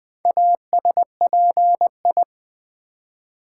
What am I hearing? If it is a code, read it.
Morse code, heard as a keyed tone, about 20 wpm, 700 Hz. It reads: ASPI
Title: Build A Word - 4 Letters At A Time - 10-Letter Words - 20wpm